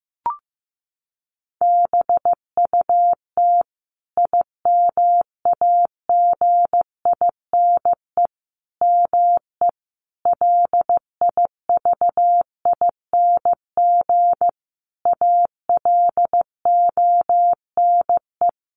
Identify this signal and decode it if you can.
Morse code, heard as a keyed tone, about 15 wpm, 700 Hz. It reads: BUT IMAGINE ME LIVING ALONE
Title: Sentences from Top 1400 Words 15wpm